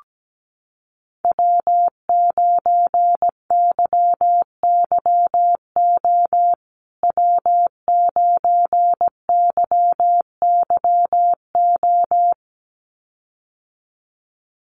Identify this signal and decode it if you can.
Morse code, heard as a keyed tone, about 17 wpm, 700 Hz. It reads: W9YYO W9YYO
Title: USA Call Signs Repeated 2x 17wpm